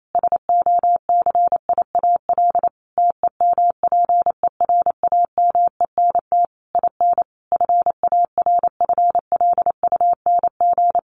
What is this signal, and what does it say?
Morse code, heard as a keyed tone, about 28 wpm, 700 Hz. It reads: SOCIAL TEMPERAMENT IN FARFLUNG
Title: A Sound Of Thunder 28wpm